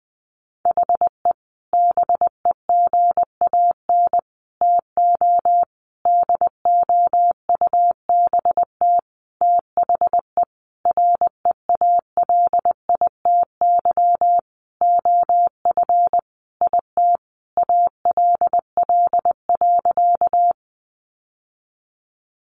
Morse code, heard as a keyed tone, about 20 wpm, 700 Hz. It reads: HE BEGAN TO DOUBT THE REALITY OF IT ALL.
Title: Sentences from Top 1500 Words 20wpm